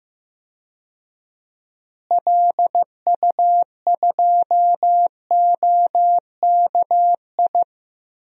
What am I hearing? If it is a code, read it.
Morse code, heard as a keyed tone, about 15 wpm, 700 Hz. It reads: LU2OKI
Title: DXCC-100 Call Signs - 6 Characters - Repeated 3x - 15wpm